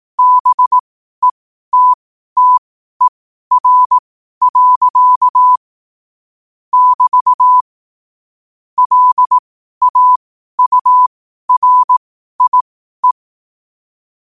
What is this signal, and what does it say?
Morse code, heard as a keyed tone, about 18 wpm, spaced as for 12 wpm, 1000 Hz. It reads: BETTER. <BT> LAURIE
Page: Quotes for Thu, 14 Aug 2025 in Morse Code at 12 words per minute.